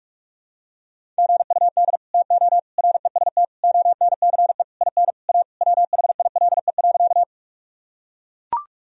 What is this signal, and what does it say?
Morse code, heard as a keyed tone, about 42 wpm, 700 Hz. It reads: GUD TO REST ONCE IN A WHILE.
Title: Sentences from Top 500 Words and Abbreviations 45wpm